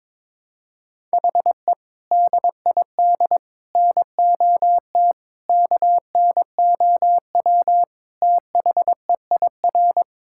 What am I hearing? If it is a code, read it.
Morse code, heard as a keyed tone, about 22 wpm, 700 Hz. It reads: HE DID NOT KNOW THEIR
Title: Sentences from Top 1000 Words - Review Part 6 - 22wpm